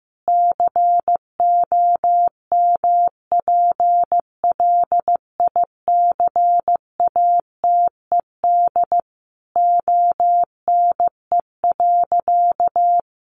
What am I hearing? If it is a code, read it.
Morse code, heard as a keyed tone, about 15 wpm, 700 Hz. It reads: COMPLICATED ONE.
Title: Sentences from Top 1200 Words 15wpm